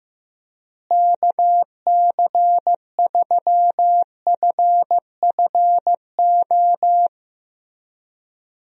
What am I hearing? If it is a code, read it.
Morse code, heard as a keyed tone, about 15 wpm, 700 Hz. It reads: KC3FFO
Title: USA Call Signs - 6 Characters - Repeated 3x - 15wpm